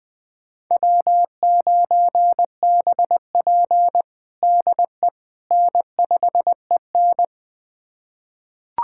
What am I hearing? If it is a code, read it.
Morse code, heard as a keyed tone, about 20 wpm, 700 Hz. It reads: W9BP DE N5EN
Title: USA Callsign DE Callsign - 4 Characters - 20wpm